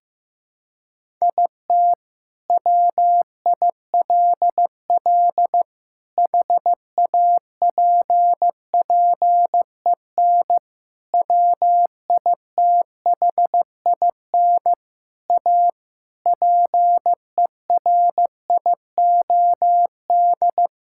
Morse code, heard as a keyed tone, about 15 wpm, 700 Hz. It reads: IT WILL HAPPEN WITHIN A PERIOD
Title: Sentences from Top 1000 Words - Review Part 5 - 15wpm